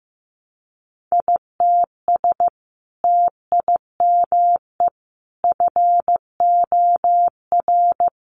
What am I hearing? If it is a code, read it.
Morse code, heard as a keyed tone, about 15 wpm, 700 Hz. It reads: ITS TIME FOR
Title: Sentences from Top 600 Words 15wpm